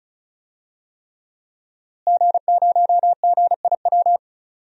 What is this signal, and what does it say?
Morse code, heard as a keyed tone, about 36 wpm, 700 Hz. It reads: G0GIW
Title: DXCC-100 Call Signs - 5 Characters - Repeated 3x - 35wpm